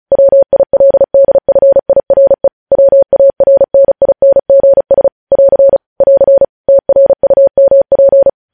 Morse code, heard as a keyed tone, about 36 wpm, 550 Hz. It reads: WILDFIRE WARNINGS <AR> <AR> TRUMP
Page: news headlines in morse code at 35 wpm